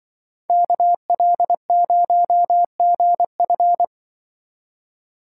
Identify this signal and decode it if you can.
Morse code, heard as a keyed tone, about 25 wpm, 700 Hz. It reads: KL0GF
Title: Speed Racing - USA 2x2 Call Signs 20wpm